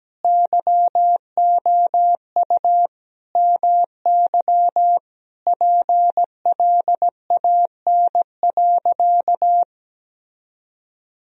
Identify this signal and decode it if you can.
Morse code, heard as a keyed tone, about 17 wpm, 700 Hz. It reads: YOU MY PLAN.
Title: Sentences from Top 1000 Words - Review Part 2 - 17wpm